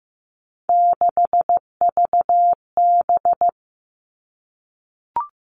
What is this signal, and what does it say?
Morse code, heard as a keyed tone, about 15 wpm, 700 Hz. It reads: 6VB
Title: All Cut Numbers - 3-Digit 15wpm